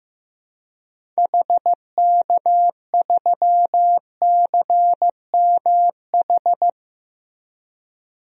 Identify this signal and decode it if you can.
Morse code, heard as a keyed tone, about 15 wpm, 700 Hz. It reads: HK3CMH
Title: DXCC-100 Call Signs - 6 Characters - Repeated 3x - 15wpm